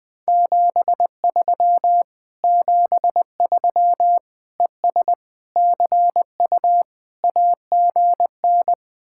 Morse code, heard as a keyed tone, about 21 wpm, 700 Hz. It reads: 73 73 ES CU AGN
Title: Speed-Racing Common QSO Phrases Encore - 20wpm